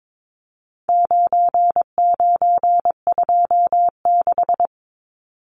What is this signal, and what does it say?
Morse code, heard as a keyed tone, about 22 wpm, 700 Hz. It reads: 9926
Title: Numbers - 4-Digit 22wpm